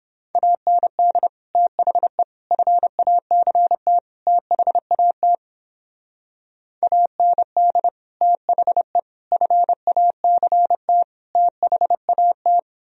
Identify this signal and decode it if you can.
Morse code, heard as a keyed tone, about 29 wpm, 700 Hz. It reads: AND THE FACT THAT AND THE FACT THAT